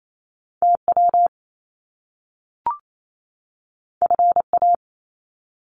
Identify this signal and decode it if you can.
Morse code, heard as a keyed tone, about 28 wpm, 700 Hz. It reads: TW FA
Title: Top Two Letter Combinations in Words 28wpm